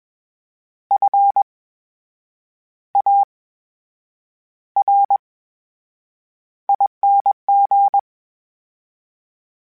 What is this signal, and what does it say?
Morse code, heard as a keyed tone, about 21 wpm, spaced as for 8 wpm, 800 Hz. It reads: FARING
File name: Special INGs-mp3s / Special INGs-mp3s / VerySlowToFaster Ing @20